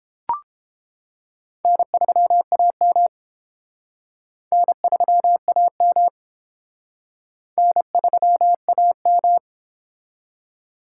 Morse code, heard as a keyed tone, about 31 wpm, 700 Hz. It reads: N3AM N3AM N3AM